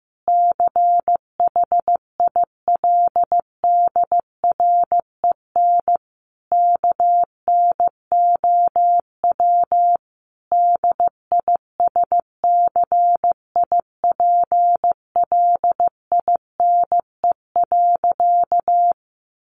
Morse code, heard as a keyed tone, about 15 wpm, 700 Hz. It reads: CHILDREN KNOW DISCIPLINE.
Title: Sentences from Top 900 Words 15wpm